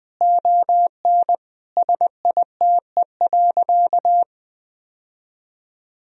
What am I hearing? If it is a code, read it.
Morse code, heard as a keyed tone, about 21 wpm, 700 Hz. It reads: ON SITE.
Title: Sentences from Top 1000 Words - Review Part 4 - 20wpm